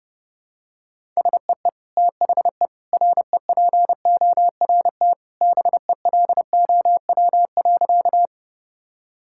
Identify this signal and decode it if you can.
Morse code, heard as a keyed tone, about 30 wpm, 700 Hz. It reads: SEE THE REPORT BELOW.
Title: Sentences from Top 1000 Words - Review Part 6 - 30wpm